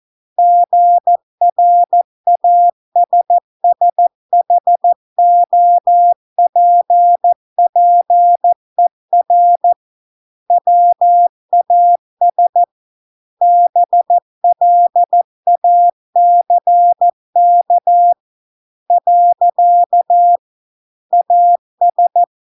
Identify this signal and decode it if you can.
Morse code, heard as a keyed tone, about 14 wpm, 700 Hz. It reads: GRASSHOPPER WAS BLACK . AS